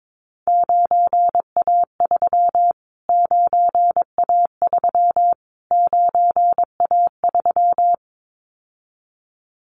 Morse code, heard as a keyed tone, about 22 wpm, 700 Hz. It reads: 9A3 9A3 9A3
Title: Callsign Prefixes for Top 25 DXCC - Repeated 3x - 22wpm